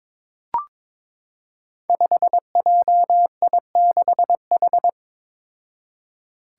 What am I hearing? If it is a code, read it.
Morse code, heard as a keyed tone, about 21 wpm, 700 Hz. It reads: 5JI6H
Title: Five Challenging Letter-Number Combinations 22wpm